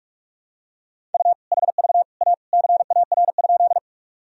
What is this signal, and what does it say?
Morse code, heard as a keyed tone, about 43 wpm, 700 Hz. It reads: U HV A CAR?
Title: Sentences from Top 300 Words and Abbreviations 45wpm